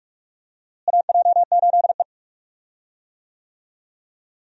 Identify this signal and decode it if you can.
Morse code, heard as a keyed tone, about 46 wpm, 700 Hz. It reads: AJ8E
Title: USA Call Signs - 4 Characters - Repeated 2x - 45wpm